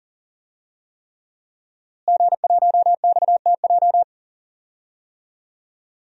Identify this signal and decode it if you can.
Morse code, heard as a keyed tone, about 40 wpm, 700 Hz. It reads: G1XTJ
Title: DXCC-100 Call Signs - 5 Characters - Repeated 3x - 40wpm